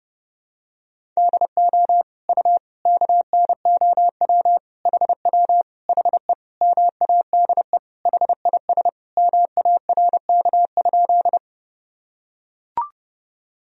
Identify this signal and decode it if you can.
Morse code, heard as a keyed tone, about 30 wpm, 700 Hz. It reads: DO U KNOW HW HE MADE HIS MARK?
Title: Sentences from Top 300 Words and Abbreviations 30wpm